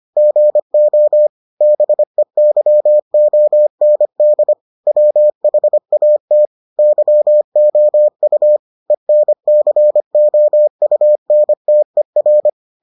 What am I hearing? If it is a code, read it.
Morse code, heard as a keyed tone, about 25 wpm, 600 Hz. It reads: GO BEYOND WHAT YOU ENCOUNTER